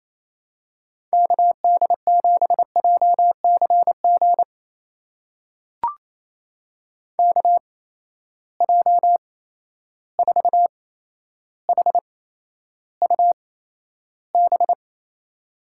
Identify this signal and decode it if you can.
Morse code, heard as a keyed tone, about 28 wpm, spaced as for 12 wpm, 700 Hz. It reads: KD7JCG KJ4HUB
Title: USA Call Signs - 6 Characters - 8x Character Spacing - 28wpm